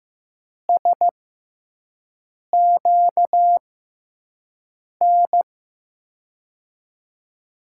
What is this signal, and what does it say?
Morse code, heard as a keyed tone, about 15 wpm, spaced as for 5 wpm, 700 Hz. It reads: SQN
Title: USA Call Signs - 6 Characters - 6x Character Spacing - 15wpm